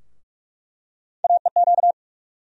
Morse code, heard as a keyed tone, about 48 wpm, 700 Hz. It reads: AEQ
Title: USA Call Sign Postfixes - 3 Letters 45wpm